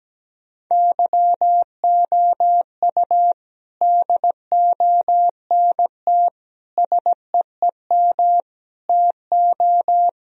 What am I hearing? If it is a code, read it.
Morse code, heard as a keyed tone, about 17 wpm, 700 Hz. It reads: YOU DONT SEEM TO
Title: Sentences from Top 1000 Words - Review Part 1 - 17wpm